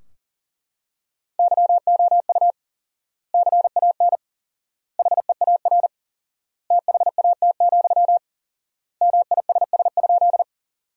Morse code, heard as a keyed tone, about 40 wpm, 700 Hz. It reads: YOU CAN HEAR THAT, MISS?
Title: 4x Word-Spacing - Sentences from Top 500 Words - 40wpm